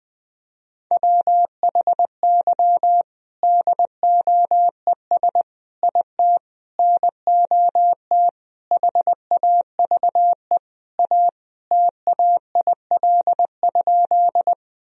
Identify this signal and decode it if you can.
Morse code, heard as a keyed tone, about 20 wpm, 700 Hz. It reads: WHY DOES IT NOT HAVE A TAIL?
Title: Sentences from Top 1000 Words - Review Part 3 - 20wpm